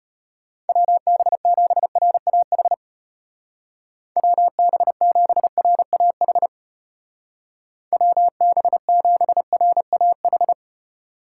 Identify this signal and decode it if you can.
Morse code, heard as a keyed tone, about 34 wpm, 700 Hz. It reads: WB7RAH WB7RAH WB7RAH